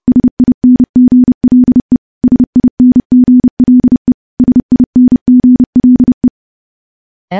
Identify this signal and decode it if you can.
Morse code, heard as a keyed tone, about 30 wpm, 250 Hz. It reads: SINGLE SINGLE SINGLE
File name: morse-training-30WPM